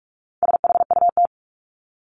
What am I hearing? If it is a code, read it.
Morse code, heard as a keyed tone, about 44 wpm, 700 Hz. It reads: SHUT
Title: Top 1751-2000 Words 45wpm